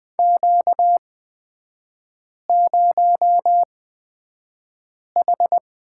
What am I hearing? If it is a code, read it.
Morse code, heard as a keyed tone, about 20 wpm, spaced as for 5 wpm, 700 Hz. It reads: Q0H
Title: 236 Lesson 38e - Letter q - Three Characters 20wpm